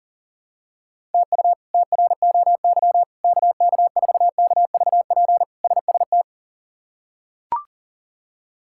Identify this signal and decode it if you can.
Morse code, heard as a keyed tone, about 40 wpm, 700 Hz. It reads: TU TROY KK4KVP SST
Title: K1USN SST QSO Practice - 40wpm